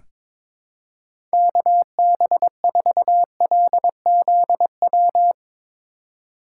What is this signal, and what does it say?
Morse code, heard as a keyed tone, about 22 wpm, 700 Hz. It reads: KB4LZW
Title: USA Call Signs - 6 Characters - 3x Character Spacing - 22wpm